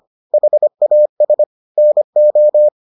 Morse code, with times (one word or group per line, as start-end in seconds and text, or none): 0.3-1.4 HAS
1.8-2.7 NO